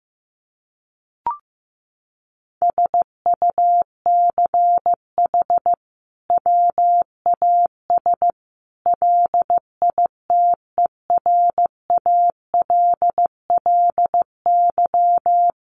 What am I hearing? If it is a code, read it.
Morse code, heard as a keyed tone, about 15 wpm, 700 Hz. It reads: SUCH WAS LITERALLY
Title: Sentences from Top 1400 Words 15wpm